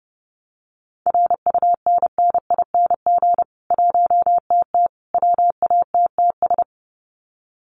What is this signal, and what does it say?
Morse code, heard as a keyed tone, about 30 wpm, 700 Hz. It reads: RUNNING 1TT WATTS
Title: Common QSO Phrases 30wpm